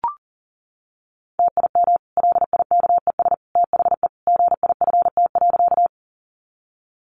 Morse code, heard as a keyed tone, about 40 wpm, 700 Hz. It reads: TIM LIKES THE GIFT.
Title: Easy Phrases 40wpm